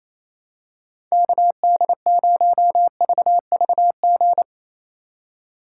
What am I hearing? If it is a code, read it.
Morse code, heard as a keyed tone, about 28 wpm, 700 Hz. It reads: KD0VVG
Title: USA Call Signs - 4x Character Spacing - 28wpm